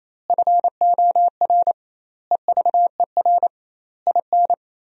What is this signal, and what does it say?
Morse code, heard as a keyed tone, about 28 wpm, 700 Hz. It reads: FOR EVER IN
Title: 2x Word-Spacing - Sentences from Top 400 Words - 28wpm